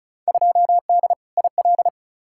Morse code, heard as a keyed tone, about 36 wpm, 700 Hz. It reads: 2D IL
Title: Field Day Exchanges - 35wpm